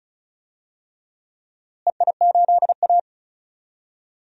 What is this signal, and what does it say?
Morse code, heard as a keyed tone, about 36 wpm, 700 Hz. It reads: EI8A
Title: DXCC-100 Call Signs - 4 Characters - Repeated 3x - 35wpm